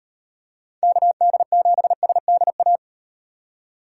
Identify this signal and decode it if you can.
Morse code, heard as a keyed tone, about 38 wpm, 700 Hz. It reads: KD7SDA